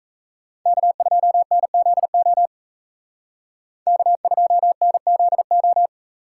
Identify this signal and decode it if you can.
Morse code, heard as a keyed tone, about 36 wpm, 700 Hz. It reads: K2NZO K2NZO